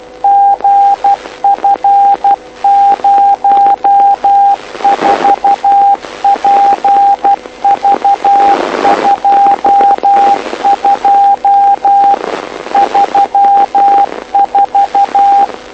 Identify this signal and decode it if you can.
Morse code, heard as a keyed tone, about 12 wpm, 800 Hz, with radio noise. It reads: GF04PV22 34
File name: lu4aao / lu7aa 09dic2011 exp planeador vuelo3